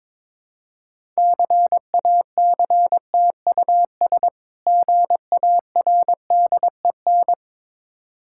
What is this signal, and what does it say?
Morse code, heard as a keyed tone, about 22 wpm, 690 Hz. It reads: CACTUS GARDEN
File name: Difficult Words with Hints - 22wpm